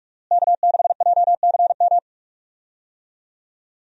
Morse code, heard as a keyed tone, about 47 wpm, 700 Hz. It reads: K6JCM
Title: USA Call Signs - 5 Characters - Repeated 2x - 45wpm